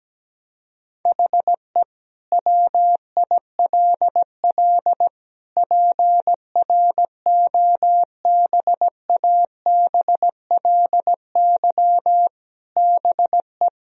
Morse code, heard as a keyed tone, about 17 wpm, 700 Hz. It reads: HE WILL PROBABLY BE
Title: Sentences from Top 1000 Words - Review Part 4 - 17wpm